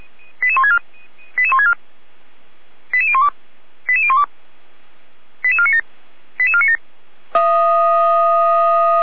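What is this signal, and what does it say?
Five-tone selective call (ZVEI), heard as quick runs of five tones, tones of about 70 ms. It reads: 88164 88164 88012 88012 88497 88497
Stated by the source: SDR-received ZVEI five-tone selective call, code 88125.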